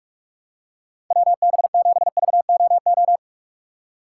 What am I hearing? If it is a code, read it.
Morse code, heard as a keyed tone, about 45 wpm, 700 Hz. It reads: WB7VOO
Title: USA Call Signs - 6x Character Spacing - 45wpm